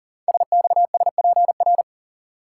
Morse code, heard as a keyed tone, about 40 wpm, 690 Hz. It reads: SXSPR
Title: Code Groups - 5 Letters - 40wpm